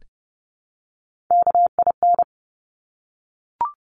Morse code, heard as a keyed tone, about 30 wpm, 700 Hz. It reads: KIN
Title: Top Three Letter Combinations in Words 30wpm